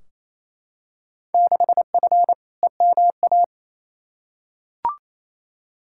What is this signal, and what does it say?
Morse code, heard as a keyed tone, about 28 wpm, 700 Hz. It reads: 6F EMA
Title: Field Day Exchanges Encore - 28wpm